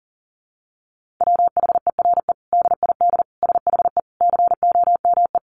Morse code, heard as a keyed tone, about 40 wpm, 700 Hz. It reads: WHERE DID SHE COME
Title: Sentences from Top 200 Words 40wpm